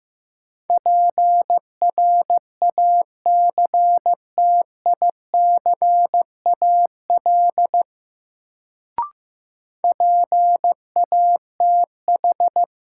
Morse code, heard as a keyed tone, about 15 wpm, 700 Hz. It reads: PRACTICAL PATH
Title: Sentences from Top 1400 Words - New Word Drill Alt - 15wpm